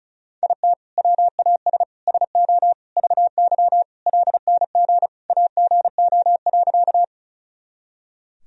Morse code, heard as a keyed tone, about 36 wpm, 700 Hz. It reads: IT WAS SO VY LNG AGO.
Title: Sentences from Top 500 Words and Abbreviations 35wpm